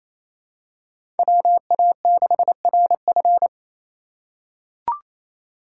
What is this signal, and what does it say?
Morse code, heard as a keyed tone, about 28 wpm, 700 Hz. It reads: WA6RF
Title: USA Call Signs - 4x Character Spacing - 28wpm